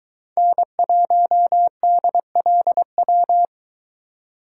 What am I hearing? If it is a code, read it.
Morse code, heard as a keyed tone, about 22 wpm, 700 Hz. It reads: N1DLW